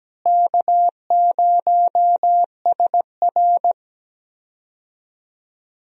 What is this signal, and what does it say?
Morse code, heard as a keyed tone, about 17 wpm, 700 Hz. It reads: K0SR
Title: USA Call Signs - 4 Characters - Repeated 2x - 17wpm